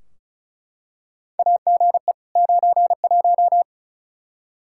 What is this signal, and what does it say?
Morse code, heard as a keyed tone, about 34 wpm, 700 Hz. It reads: AGE 91
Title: Practice QSO - Exchange 3 - More Info - 35wpm